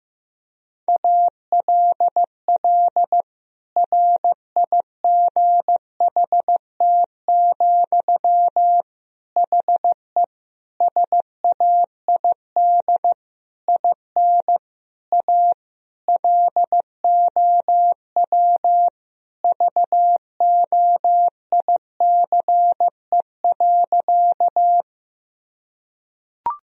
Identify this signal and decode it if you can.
Morse code, heard as a keyed tone, about 15 wpm, 700 Hz. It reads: ALL RIGHT, HE SAID IN A LOW VOICE.
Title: Sentences from Top 1000 Words - Review Part 1 - 15wpm